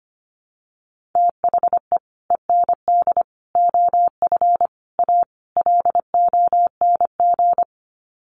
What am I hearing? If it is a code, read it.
Morse code, heard as a keyed tone, about 25 wpm, 700 Hz. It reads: THE END OF A LONG
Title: Sets of 5 Words Encore2 25wpm